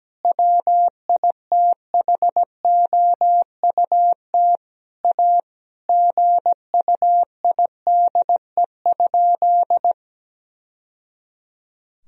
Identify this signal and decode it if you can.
Morse code, heard as a keyed tone, about 17 wpm, 700 Hz. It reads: WITHOUT A GUIDE?
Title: Sentences from Top 1000 Words - Review Part 3 - 17wpm